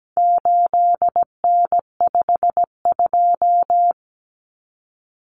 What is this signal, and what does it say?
Morse code, heard as a keyed tone, about 17 wpm, 700 Hz. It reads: 8N52
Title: Common Cut Numbers - 4-Digit 17wpm